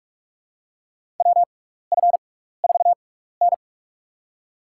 Morse code, heard as a keyed tone, about 46 wpm, spaced as for 16 wpm, 700 Hz. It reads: WF4N
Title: USA Call Signs - 6x Character Spacing - 45wpm